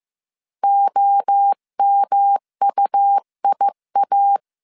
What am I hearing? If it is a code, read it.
Morse code, heard as a keyed tone, about 15 wpm, 800 Hz. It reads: OMUIA
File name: CW / DET / 1 3 1500C S C 14wpm 2458